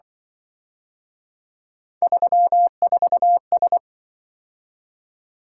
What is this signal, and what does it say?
Morse code, heard as a keyed tone, about 25 wpm, 700 Hz. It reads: 34S
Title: Speed-Racing - Three Letter-Number Combinations - 20wpm